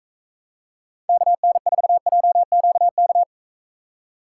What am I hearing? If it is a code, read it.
Morse code, heard as a keyed tone, about 44 wpm, 700 Hz. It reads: KN4JQK